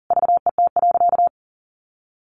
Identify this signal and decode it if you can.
Morse code, heard as a keyed tone, about 40 wpm, 700 Hz. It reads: VET.
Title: Easy Phrases 40wpm